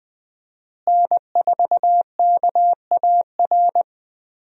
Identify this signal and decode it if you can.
Morse code, heard as a keyed tone, about 20 wpm, 700 Hz. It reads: N4KAR
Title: USA Call Signs - 5 Characters - Repeated 2x - 20wpm